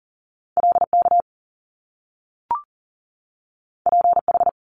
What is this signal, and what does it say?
Morse code, heard as a keyed tone, about 40 wpm, 700 Hz. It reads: LK PH
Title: Top Two Letter Combinations in Words 40wpm